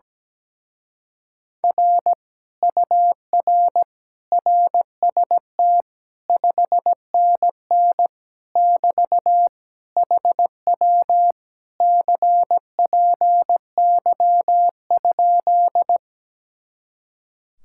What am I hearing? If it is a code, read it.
Morse code, heard as a keyed tone, about 17 wpm, 700 Hz. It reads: R UR RST 5NN <BT> HW CPY?
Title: Speed-Racing Common QSO Phrases - 17wpm